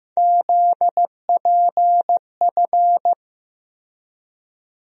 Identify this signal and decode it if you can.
Morse code, heard as a keyed tone, about 15 wpm, 700 Hz. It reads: ZPF
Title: USA Call Sign Postfixes - 3 Letters 15wpm